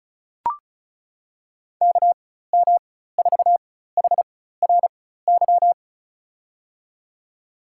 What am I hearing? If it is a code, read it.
Morse code, heard as a keyed tone, about 34 wpm, spaced as for 16 wpm, 700 Hz. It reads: KM4HRY
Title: USA Call Signs - 4x Character Spacing - 35wpm